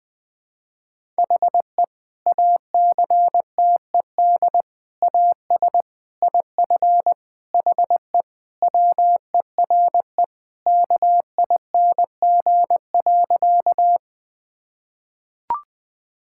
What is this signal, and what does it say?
Morse code, heard as a keyed tone, about 20 wpm, 700 Hz. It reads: HE ACTED AS IF HE WERE KING.
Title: Sentences with 4-Word Phrases - 20wpm